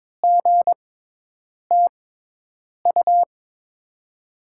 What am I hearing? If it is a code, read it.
Morse code, heard as a keyed tone, about 22 wpm, spaced as for 8 wpm, 700 Hz. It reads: GTU
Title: USA Call Signs - 6 Characters - 6x Character Spacing - 22wpm